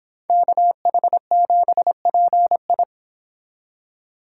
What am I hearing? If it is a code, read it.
Morse code, heard as a keyed tone, about 30 wpm, 700 Hz. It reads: KH7PI